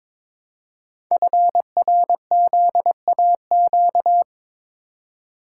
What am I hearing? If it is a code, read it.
Morse code, heard as a keyed tone, about 22 wpm, 700 Hz. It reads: FRZAQ
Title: Code Groups - 5 Letters - 22wpm